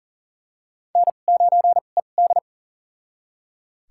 Morse code, heard as a keyed tone, about 40 wpm, spaced as for 29 wpm, 690 Hz. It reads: N9ED
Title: USA Call Signs - 4 Characters - 2x Character Spacing - 40wpm